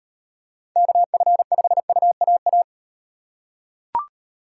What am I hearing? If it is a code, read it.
Morse code, heard as a keyed tone, about 38 wpm, 700 Hz. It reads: KF5UAA